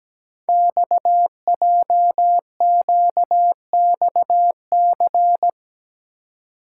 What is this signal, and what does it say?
Morse code, heard as a keyed tone, about 17 wpm, 700 Hz. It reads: XJQXC
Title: Code Groups - 5 Letters - 17wpm